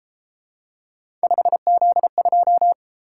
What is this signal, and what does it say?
Morse code, heard as a keyed tone, about 29 wpm, 700 Hz. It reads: 5Z2